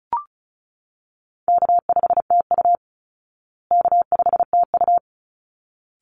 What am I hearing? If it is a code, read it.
Morse code, heard as a keyed tone, about 34 wpm, 700 Hz. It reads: K5TU K5TU
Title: Call Signs Repeated 3x - 35wpm